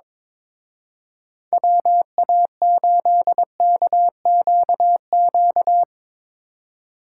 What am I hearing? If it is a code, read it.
Morse code, heard as a keyed tone, about 22 wpm, 700 Hz. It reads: WA8KQQ
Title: Speed Racing - USA Call Signs 22wpm